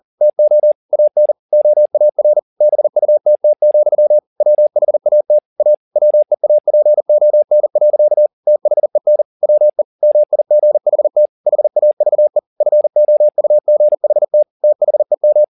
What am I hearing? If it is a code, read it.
Morse code, heard as a keyed tone, about 40 wpm, 600 Hz. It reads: TO AN OAR BUTT, WHAT A WEAPON. THEN WE MIGHT HAVE FOUGHT THEM
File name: the old man and the sea 2 40wpm